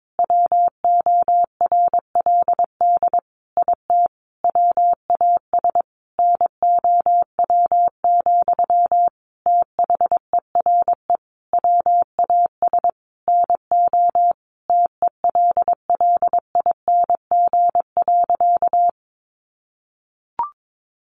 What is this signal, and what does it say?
Morse code, heard as a keyed tone, about 22 wpm, 700 Hz. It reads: WORLD IT WAS NOW, THERE WAS NO TELLING.
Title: A Sound Of Thunder 22wpm